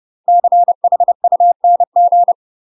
Morse code, heard as a keyed tone, about 30 wpm, 700 Hz. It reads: CHUNG
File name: UK MASTER QSOs TRAINING / 30WPM / UK MASTER QSOs 0030